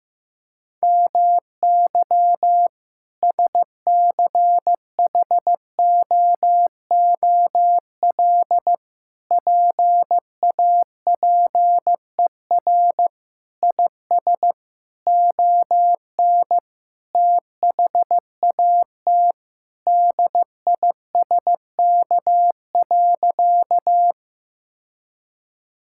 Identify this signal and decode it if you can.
Morse code, heard as a keyed tone, about 15 wpm, 700 Hz. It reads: MY SCHOOL PAPER IS ON THAT DISK.
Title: Sentences from Top 1000 Words - Review Part 6 - 15wpm